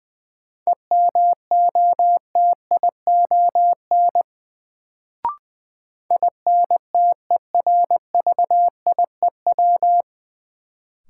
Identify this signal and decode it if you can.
Morse code, heard as a keyed tone, about 21 wpm, 700 Hz. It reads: EMOTION INTERVIEW
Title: Sentences from Top 1600 Words - New Word Drill Alt - 20wpm